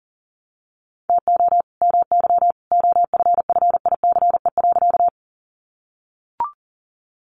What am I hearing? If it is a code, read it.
Morse code, heard as a keyed tone, about 40 wpm, 700 Hz. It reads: TO MY OFFICE.
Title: Sentences from Top 1000 Words 40wpm